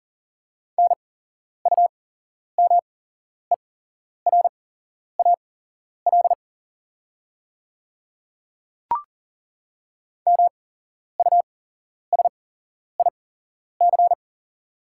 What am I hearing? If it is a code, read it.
Morse code, heard as a keyed tone, about 40 wpm, spaced as for 11 wpm, 690 Hz. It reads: NUMERAL MUSIC
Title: 8x Character Spacing - Top 500 Words - 40wpm